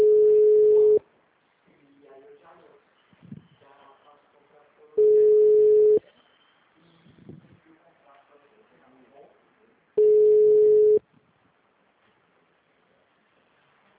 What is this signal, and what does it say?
Ringback tone (Europe, 425 Hz), with speech.
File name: out- -2011- - - .168